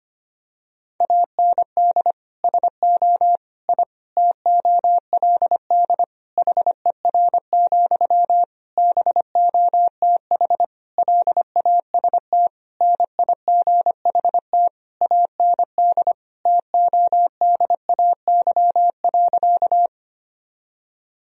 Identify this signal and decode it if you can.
Morse code, heard as a keyed tone, about 25 wpm, 700 Hz. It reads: AND SO I TOLD HER, BOTH LAST NIGHT AND TODAY.
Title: Sentences from Top 1000 Words - Review Part 4 - 25wpm